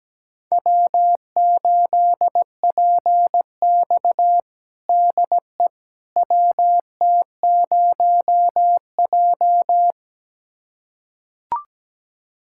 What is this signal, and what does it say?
Morse code, heard as a keyed tone, about 17 wpm, 700 Hz. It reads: W8PX DE WT0J
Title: USA Callsign DE Callsign - 4 Characters - 17wpm